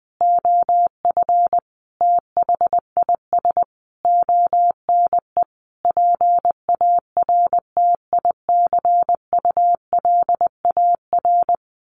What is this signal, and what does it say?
Morse code, heard as a keyed tone, about 20 wpm, 700 Hz. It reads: OF THIS ONE PARTICULAR
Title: A Sound Of Thunder 20wpm